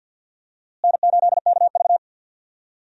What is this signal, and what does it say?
Morse code, heard as a keyed tone, about 53 wpm, 700 Hz. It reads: N8KV
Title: USA Call Signs - 4 Characters - Repeated 3x - 50wpm